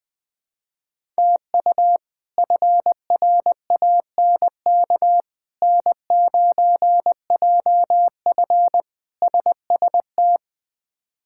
Morse code, heard as a keyed tone, about 20 wpm, 700 Hz. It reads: TU FRANK N9JF SST
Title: K1USN SST QSO Practice - 20wpm